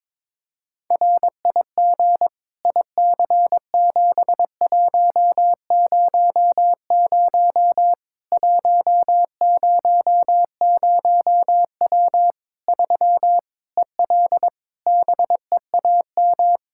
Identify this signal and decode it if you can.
Morse code, heard as a keyed tone, about 22 wpm, 700 Hz. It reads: RIG IC7100 100W 3 EL BEAM
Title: Practice QSO - Exchange 2 - Info - 22wpm